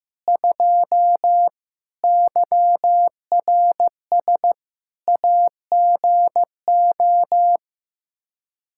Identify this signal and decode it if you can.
Morse code, heard as a keyed tone, about 15 wpm, 700 Hz. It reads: 2 YRS AGO
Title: Practice QSO - Exchange 3 - More Info - 15wpm